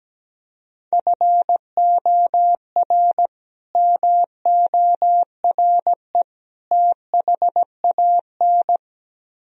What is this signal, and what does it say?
Morse code, heard as a keyed tone, about 17 wpm, 700 Hz. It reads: FOR MORE THAN
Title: Speed-Racing - Sets of 3 Words 17wpm